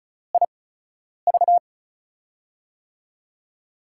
Morse code, heard as a keyed tone, about 36 wpm, spaced as for 10 wpm, 700 Hz. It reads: IV
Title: USA Call Signs - 8x Character Spacing - 35wpm